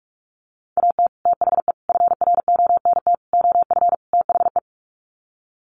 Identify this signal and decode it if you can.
Morse code, heard as a keyed tone, about 44 wpm, 700 Hz. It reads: AT THE FRONT OF THE
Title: Sets of 5 Words 45wpm